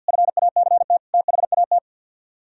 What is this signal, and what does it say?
Morse code, heard as a keyed tone, about 64 wpm, 700 Hz. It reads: FACT THAT